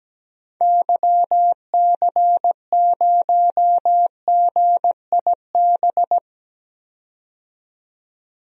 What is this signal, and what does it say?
Morse code, heard as a keyed tone, about 17 wpm, 700 Hz. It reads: YC0GIB
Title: DXCC-100 Call Signs - 6 Characters - 17wpm